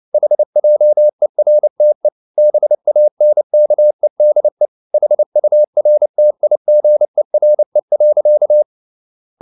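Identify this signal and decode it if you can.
Morse code, heard as a keyed tone, about 29 wpm, 600 Hz. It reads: HJERTE BANKEDE HURTIGERE.
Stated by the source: Fast (QRQ) Morse at 29 wpm, from CW med Gnister.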